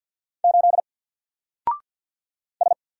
Morse code, heard as a keyed tone, about 47 wpm, 700 Hz. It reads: 8 S
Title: Single Letter-Number 50wpm